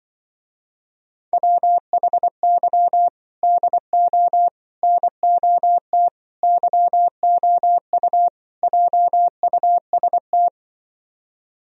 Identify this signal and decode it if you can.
Morse code, heard as a keyed tone, about 24 wpm, 700 Hz. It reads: WHY DO NOT YOU JUST
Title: Speed-Racing - Sets of 5 Words 20wpm